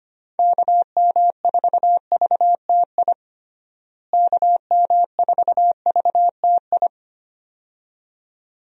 Morse code, heard as a keyed tone, about 25 wpm, 700 Hz. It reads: KM4VTI KM4VTI
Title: USA Call Signs - 6 Characters - Repeated 2x - 25wpm